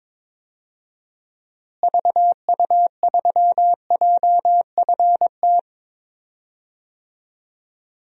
Morse code, heard as a keyed tone, about 22 wpm, 700 Hz. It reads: VU3JFT
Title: DXCC-100 Call Signs - 6 Characters - Repeated 3x - 22wpm